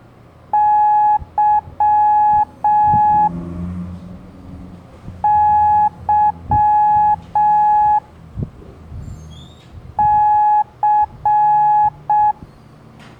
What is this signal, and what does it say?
Morse code, heard as a keyed tone, about 6 wpm, 840 Hz. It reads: Y Y C
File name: blog / LETRAS COM SOM SEMELHANTE MORSE